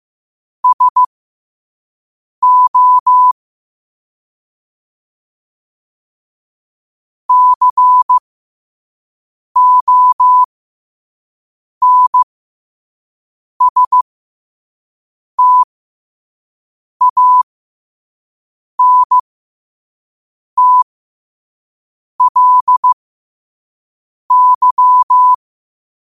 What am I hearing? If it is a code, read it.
Morse code, heard as a keyed tone, about 15 wpm, spaced as for 5 wpm, 1000 Hz. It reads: SO CONSTANTLY
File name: qotdfiles / 5wpm